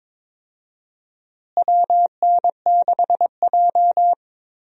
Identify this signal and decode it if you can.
Morse code, heard as a keyed tone, about 22 wpm, 700 Hz. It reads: WN6J
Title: USA Call Signs - 4 Characters - Repeated 2x - 22wpm